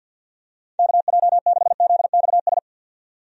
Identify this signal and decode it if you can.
Morse code, heard as a keyed tone, about 53 wpm, 700 Hz. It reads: KJ6ZXS
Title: USA Call Signs - 6 Characters - 50wpm